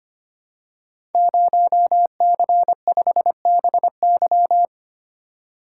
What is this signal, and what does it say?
Morse code, heard as a keyed tone, about 25 wpm, 700 Hz. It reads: 0C5BY
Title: Code Groups - 5 Letter:Numbers - 25wpm